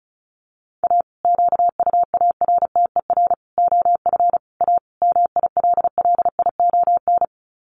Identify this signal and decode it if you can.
Morse code, heard as a keyed tone, about 34 wpm, 700 Hz. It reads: A QUARTER OF A MILLION
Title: Sets of 5 Words Encore2 35wpm